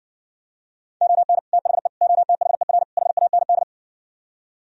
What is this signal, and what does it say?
Morse code, heard as a keyed tone, about 60 wpm, 700 Hz. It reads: ON THE OTHER HAND